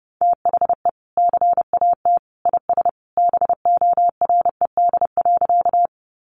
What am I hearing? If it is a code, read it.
Morse code, heard as a keyed tone, about 30 wpm, 700 Hz. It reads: THE CAT IS BORED.
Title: Easy Phrases 30wpm